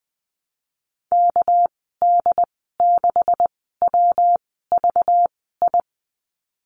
Morse code, heard as a keyed tone, about 19 wpm, spaced as for 14 wpm, 700 Hz. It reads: KD6WVI
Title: 200 Lesson 32g - Letter k - Call Signs 20wpm